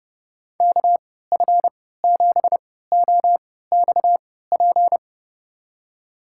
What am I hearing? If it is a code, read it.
Morse code, heard as a keyed tone, about 32 wpm, spaced as for 17 wpm, 700 Hz. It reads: KF7OXP
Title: USA Call Signs - 6 Characters - 3x Character Spacing - 30wpm